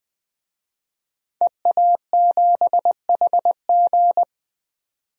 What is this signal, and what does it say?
Morse code, heard as a keyed tone, about 20 wpm, 700 Hz. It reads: EA7HG
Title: DXCC-100 Call Signs - 5 Characters - 20wpm